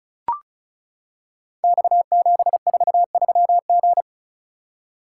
Morse code, heard as a keyed tone, about 36 wpm, 690 Hz. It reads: X743G
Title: Code Groups - 5 Letter:Numbers - 35wpm